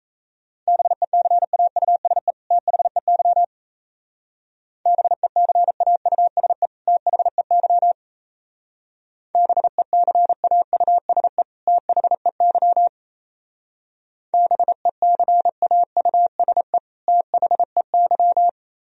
Morse code, heard as a keyed tone, about 35 wpm, 690 Hz. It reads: BECAUSE THEY BECAUSE THEY BECAUSE THEY BECAUSE THEY
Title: Speed-Racing - Sets of 2 Words 28wpm